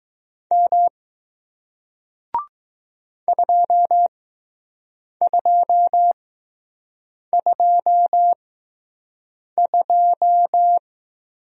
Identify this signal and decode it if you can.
Morse code, heard as a keyed tone, about 20 wpm, spaced as for 6 wpm, 700 Hz. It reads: M 2222